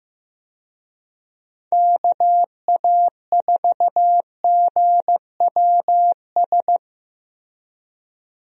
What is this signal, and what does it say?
Morse code, heard as a keyed tone, about 15 wpm, 700 Hz. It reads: KA4GWS
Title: USA Call Signs - 6 Characters - Repeated 2x - 15wpm